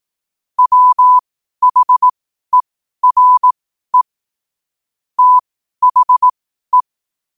Morse code, heard as a keyed tone, about 18 wpm, spaced as for 12 wpm, 1000 Hz. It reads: WHERE THE
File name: qotdfiles / 12wpm